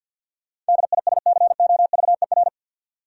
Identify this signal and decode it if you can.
Morse code, heard as a keyed tone, about 52 wpm, 700 Hz. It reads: DISCOVER
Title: Sentences from Top 1100 Words - New Word Drill Alt - 50wpm